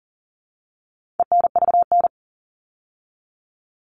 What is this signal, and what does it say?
Morse code, heard as a keyed tone, about 40 wpm, 700 Hz. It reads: ENVN
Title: All Cut Numbers - 4-Digit 40wpm